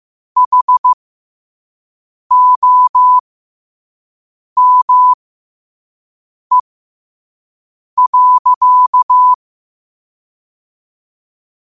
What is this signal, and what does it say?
Morse code, heard as a keyed tone, about 15 wpm, spaced as for 5 wpm, 1000 Hz. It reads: HOME.
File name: qotdfiles / 5wpm